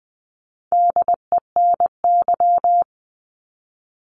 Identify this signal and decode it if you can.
Morse code, heard as a keyed tone, about 20 wpm, 700 Hz. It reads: DENY
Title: 260 Lesson 42e - Words 20-20wpm